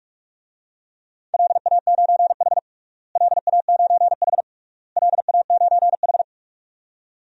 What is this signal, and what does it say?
Morse code, heard as a keyed tone, about 46 wpm, 700 Hz. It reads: LA9H LA9H LA9H
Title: DXCC-100 Call Signs - 4 Characters - Repeated 3x - 45wpm